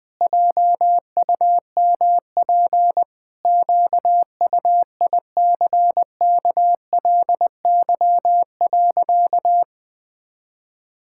Morse code, heard as a keyed tone, about 21 wpm, 700 Hz. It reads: JUMP QUICKLY.
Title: Pangram Challenge v2 - 20wpm